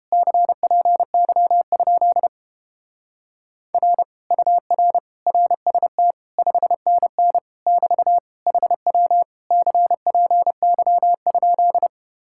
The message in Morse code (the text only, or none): CPY? R UR RST 5NN <BT> HW CPY?